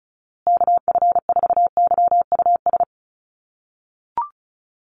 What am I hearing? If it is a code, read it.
Morse code, heard as a keyed tone, about 34 wpm, 700 Hz. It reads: KF4YUS
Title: USA Call Signs Repeated 3x 35wpm